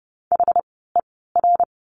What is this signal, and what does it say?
Morse code, heard as a keyed tone, about 30 wpm, spaced as for 17 wpm, 700 Hz. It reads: HER
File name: course / 30:10wpm / 108 Lesson 18f - Letter m - Words 30wpm